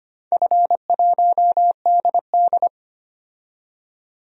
Morse code, heard as a keyed tone, about 25 wpm, 700 Hz. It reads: F1DD
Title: DXCC-100 Call Signs - 4 Characters - 25wpm